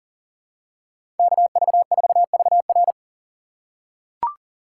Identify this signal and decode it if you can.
Morse code, heard as a keyed tone, about 40 wpm, 700 Hz. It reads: KV4VR
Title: USA Call Signs - 5 Characters - 3x Character Spacing - 40wpm